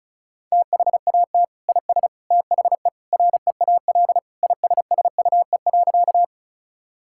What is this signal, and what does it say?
Morse code, heard as a keyed tone, about 36 wpm, 700 Hz. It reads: THAT IS THE REAL ISSUE.
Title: Sentences from Top 1000 Words - Review Part 6 - 35wpm